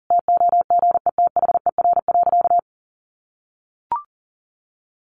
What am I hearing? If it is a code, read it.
Morse code, heard as a keyed tone, about 40 wpm, 700 Hz. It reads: TOGETHER.
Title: Sentences from Top 1100 Words 40wpm